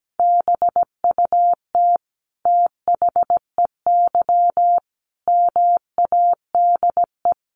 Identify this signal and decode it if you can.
Morse code, heard as a keyed tone, about 17 wpm, 700 Hz. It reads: BUT THEY MADE
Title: Sentences from Top 1300 Words 17wpm